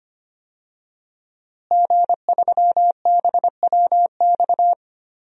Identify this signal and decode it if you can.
Morse code, heard as a keyed tone, about 25 wpm, 700 Hz. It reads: G3BWX
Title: DXCC-100 Call Signs - 5 Characters - Repeated 3x - 25wpm